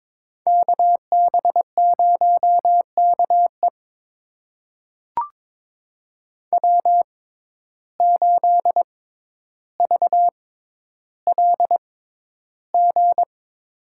Morse code, heard as a keyed tone, about 21 wpm, spaced as for 11 wpm, 700 Hz. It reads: KB0KE W8VLG
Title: USA Call Signs - 5 Characters - 6x Character Spacing - 22wpm